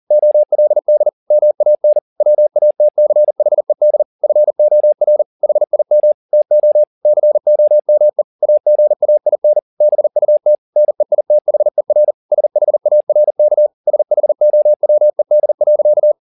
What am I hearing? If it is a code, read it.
Morse code, heard as a keyed tone, about 40 wpm, 600 Hz. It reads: OLD MAN WATCHED FOR HIM TO COME AGAIN BUT NEITHER SHARK SHOWED.
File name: the old man and the sea 2 40wpm